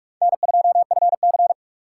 Morse code, heard as a keyed tone, about 44 wpm, 700 Hz. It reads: N2FC